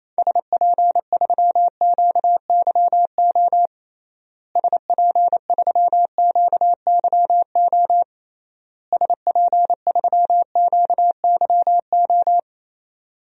Morse code, heard as a keyed tone, about 28 wpm, 700 Hz. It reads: SP3QYO SP3QYO SP3QYO
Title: DXCC-100 Call Signs - 6 Characters - Repeated 3x - 28wpm